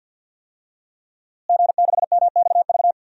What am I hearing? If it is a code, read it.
Morse code, heard as a keyed tone, about 47 wpm, 700 Hz. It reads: G6MXV
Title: DXCC-100 Call Signs - 5 Characters - Repeated 3x - 50wpm